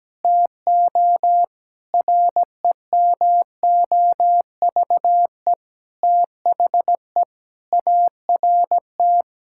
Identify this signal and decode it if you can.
Morse code, heard as a keyed tone, about 17 wpm, 700 Hz. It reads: TO REMOVE THE ART
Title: Sentences from Top 1000 Words - Review Part 6 - 17wpm